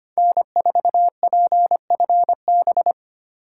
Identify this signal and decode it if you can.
Morse code, heard as a keyed tone, about 33 wpm, 700 Hz. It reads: N4PFB